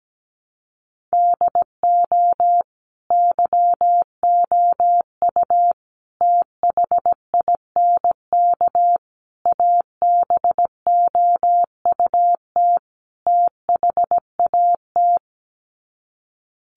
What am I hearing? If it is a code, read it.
Morse code, heard as a keyed tone, about 17 wpm, 700 Hz. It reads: DO YOU THINK ABOUT THAT
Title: Sets of 5 Words Encore2 17wpm